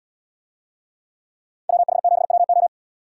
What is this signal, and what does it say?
Morse code, heard as a keyed tone, about 73 wpm, 700 Hz. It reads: WH7WO